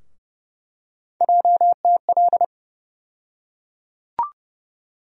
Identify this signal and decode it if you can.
Morse code, heard as a keyed tone, about 30 wpm, 700 Hz. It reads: JTL
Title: USA Call Sign Postfixes - 3 Letters 30wpm